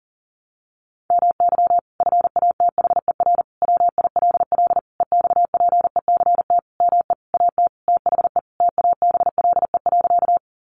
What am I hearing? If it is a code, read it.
Morse code, heard as a keyed tone, about 40 wpm, 700 Hz. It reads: MY FATHER WILL EXPECT ME AT THE TABLE.
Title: Sentences from Top 1100 Words 40wpm